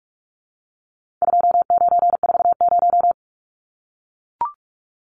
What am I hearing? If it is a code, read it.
Morse code, heard as a keyed tone, about 44 wpm, 700 Hz. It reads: 2940
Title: Numbers - 4-Digit 45wpm